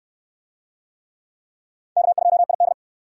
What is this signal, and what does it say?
Morse code, heard as a keyed tone, about 68 wpm, 700 Hz. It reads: K2ID